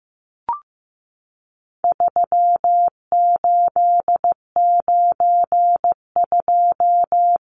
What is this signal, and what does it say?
Morse code, heard as a keyed tone, about 15 wpm, 700 Hz. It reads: 3892
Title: Numbers - 4-Digit 15wpm